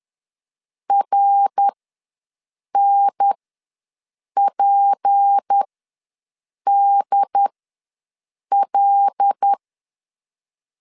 Morse code, heard as a keyed tone, about 10 wpm, spaced as for 6 wpm, 800 Hz. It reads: RNPDL
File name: LEZIONI / 6 PKX0 / 6 P ws 10wpm 3751